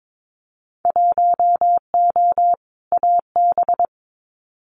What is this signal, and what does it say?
Morse code, heard as a keyed tone, about 22 wpm, 700 Hz. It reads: 1O AB
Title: Winter Field Day Exchanges Encore 22wpm